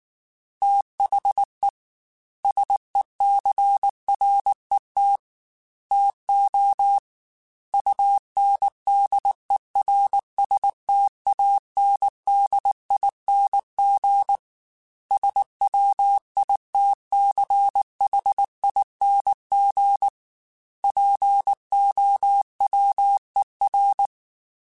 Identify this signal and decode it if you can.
Morse code, heard as a keyed tone, about 19 wpm, 800 Hz. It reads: THE SECRET TO UNDERSTANDING SWITCHING POWER
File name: Technical Subjects / Technical Subjects-mp3s / Sw Pwr Sup @18